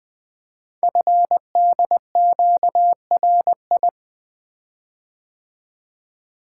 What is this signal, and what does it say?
Morse code, heard as a keyed tone, about 20 wpm, 700 Hz. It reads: FDQRI
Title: Code Groups - 5 Letters - 20wpm